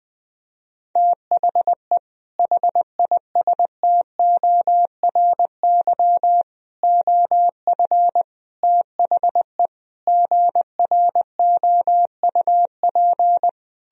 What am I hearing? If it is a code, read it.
Morse code, heard as a keyed tone, about 21 wpm, 700 Hz. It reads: THE HISTORY OF THE GROUP
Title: Sentences from Top 1000 Words - Review Part 1 - 20wpm